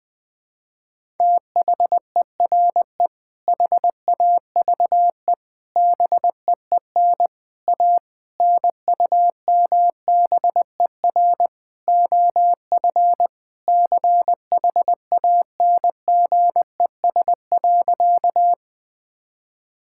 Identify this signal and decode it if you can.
Morse code, heard as a keyed tone, about 20 wpm, 700 Hz. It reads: THERE HAVE BEEN A NUMBER OF CHANGES.
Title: Sentences with 6-Word Phrases - 20wpm